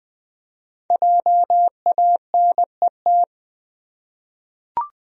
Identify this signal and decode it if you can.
Morse code, heard as a keyed tone, about 20 wpm, 700 Hz. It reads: JANET
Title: USA Common Names - 20wpm